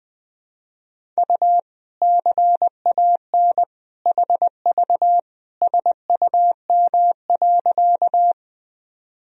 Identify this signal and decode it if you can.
Morse code, heard as a keyed tone, about 20 wpm, 700 Hz. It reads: U CAN HV SUM.
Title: Sentences from Top 100 Words and Abbreviations 20wpm